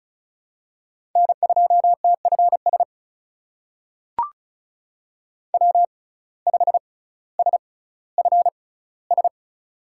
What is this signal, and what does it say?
Morse code, heard as a keyed tone, about 36 wpm, spaced as for 18 wpm, 700 Hz. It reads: N2TFS W5SFS
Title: USA Call Signs - 5 Characters - 6x Character Spacing - 35wpm